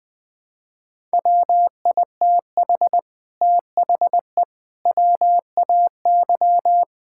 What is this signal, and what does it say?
Morse code, heard as a keyed tone, about 20 wpm, 700 Hz. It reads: WITH THE WAY
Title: Sets of 8 Words - 20wpm